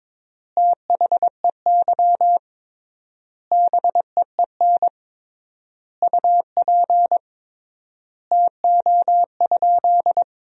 Morse code, heard as a keyed tone, about 21 wpm, 700 Hz. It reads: THEY BEEN UP TO?
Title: 3x Word-Spacing - Sentences from Top 100 Words - 22wpm